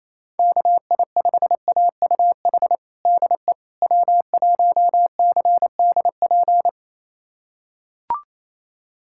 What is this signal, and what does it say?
Morse code, heard as a keyed tone, about 27 wpm, 700 Hz. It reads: KI5AUH DE W1CDP
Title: USA Callsign DE Callsign 28wpm